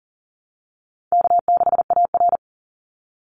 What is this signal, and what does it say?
Morse code, heard as a keyed tone, about 40 wpm, 700 Hz. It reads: K6AR
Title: Call Signs Repeated 2x - 40wpm